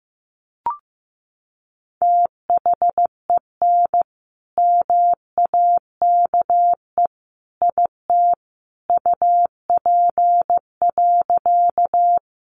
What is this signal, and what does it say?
Morse code, heard as a keyed tone, about 15 wpm, 700 Hz. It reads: THEN MAKE IT UP.
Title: Sentences from Top 100 Words 15wpm